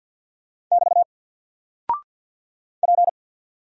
Morse code, heard as a keyed tone, about 50 wpm, 700 Hz. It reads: <BT> P
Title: Single Character (Letter:Number:Punctuation:Prosign) Alt 50wpm